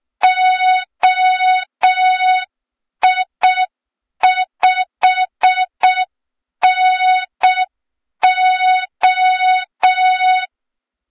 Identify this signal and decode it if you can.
Morse code, heard as a keyed tone, about 6 wpm, 3700 Hz. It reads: OI5NO
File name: Morze-HA2MN / hang / 23 GYAKORLAT